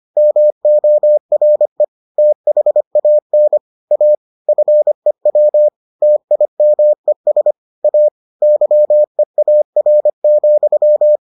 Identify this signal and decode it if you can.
Morse code, heard as a keyed tone, about 25 wpm, 600 Hz. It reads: MORE THAN A FEW TIMES A YEAR,